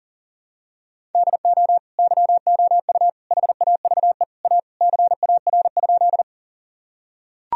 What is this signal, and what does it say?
Morse code, heard as a keyed tone, about 40 wpm, 700 Hz. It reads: DO YOU HAVE A CAR?
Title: Sentences from Top 1000 Words - Review Part 1 - 40wpm